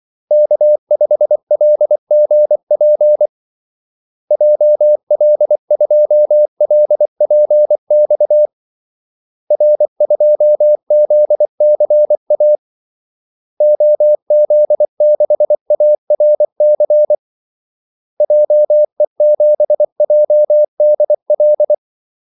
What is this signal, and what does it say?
Morse code, heard as a keyed tone, about 24 wpm, 600 Hz. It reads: K5LGP JL2LPX R2ZCA OZ6ARC JE7JDL
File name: Callsigns-24wpm